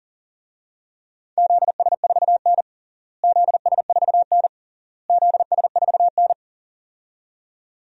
Morse code, heard as a keyed tone, about 40 wpm, 700 Hz. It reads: ZS4N ZS4N ZS4N
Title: DXCC-100 Call Signs - 4 Characters - Repeated 3x - 40wpm